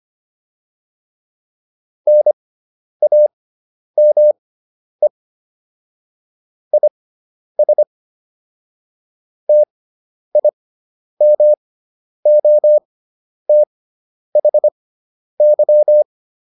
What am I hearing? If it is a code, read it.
Morse code, heard as a keyed tone, about 25 wpm, spaced as for 10 wpm, 600 Hz. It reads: NAME IS TIMOTHY